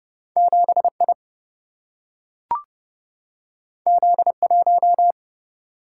Morse code, heard as a keyed tone, about 30 wpm, 700 Hz. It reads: 7I Z1
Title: Single-Letter Single-Number 30wpm